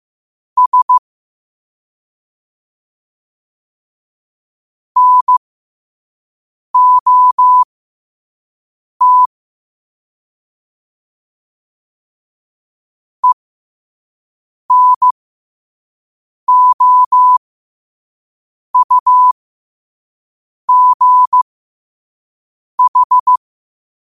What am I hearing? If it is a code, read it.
Morse code, heard as a keyed tone, about 15 wpm, spaced as for 5 wpm, 1000 Hz. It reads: S NOT ENOUGH
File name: qotdfiles / 5wpm